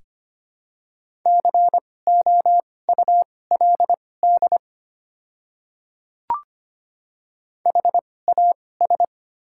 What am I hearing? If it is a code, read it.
Morse code, heard as a keyed tone, about 25 wpm, spaced as for 18 wpm, 700 Hz. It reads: COULD HAS
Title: 2x Character Spacing - Top 100 Words - 25wpm